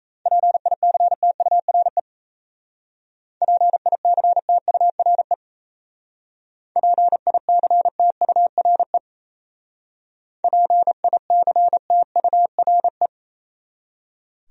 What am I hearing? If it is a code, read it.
Morse code, heard as a keyed tone, about 34 wpm, 700 Hz. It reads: PICTURE PICTURE PICTURE PICTURE
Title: Speed-Racing Top 500 Words 28wpm